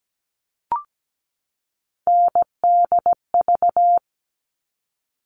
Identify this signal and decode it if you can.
Morse code, heard as a keyed tone, about 17 wpm, 700 Hz. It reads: NDV
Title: All Cut Numbers - 3-Digit 17wpm